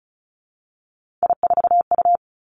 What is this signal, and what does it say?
Morse code, heard as a keyed tone, about 34 wpm, 700 Hz. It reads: I4U
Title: Three Challenging Letter-Number Combinations 35wpm